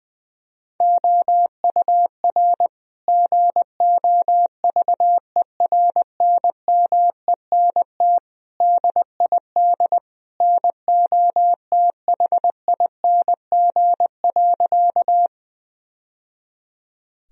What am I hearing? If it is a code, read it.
Morse code, heard as a keyed tone, about 21 wpm, 700 Hz. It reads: OUR GOVERNMENT DID NOTHING.
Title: Sentences from Top 1000 Words - Review Part 3 - 20wpm